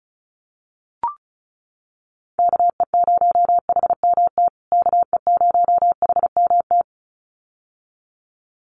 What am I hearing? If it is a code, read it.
Morse code, heard as a keyed tone, about 34 wpm, 700 Hz. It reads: KE0HMT KE0HMT
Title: USA Call Signs Repeated 2x 35wpm